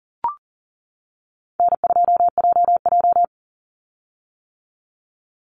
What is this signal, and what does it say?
Morse code, heard as a keyed tone, about 40 wpm, 700 Hz. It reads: N2JJ
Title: Call Signs - 40wpm